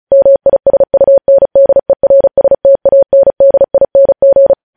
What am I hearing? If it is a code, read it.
Morse code, heard as a keyed tone, about 36 wpm, 550 Hz. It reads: MISUNDERSTANDING
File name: podcast / News headlines-1x-35x35